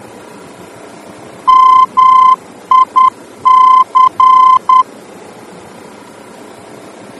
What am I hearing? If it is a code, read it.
Morse code, heard as a keyed tone, about 10 wpm, 1030 Hz. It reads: MIC
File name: germany / mx / MIC-D-VOR